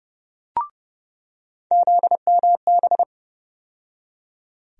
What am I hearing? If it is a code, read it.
Morse code, heard as a keyed tone, about 30 wpm, 700 Hz. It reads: ZMB
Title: USA Call Sign Postfixes - 3 Letters 30wpm